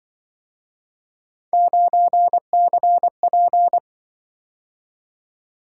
Morse code, heard as a keyed tone, about 25 wpm, 700 Hz. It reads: 9CP
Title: Speed-Racing - Three Letter-Number Combinations - 20wpm